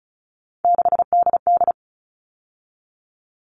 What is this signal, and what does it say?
Morse code, heard as a keyed tone, about 34 wpm, 700 Hz. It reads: 6DD
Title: All Cut Numbers - 3-Digit 35wpm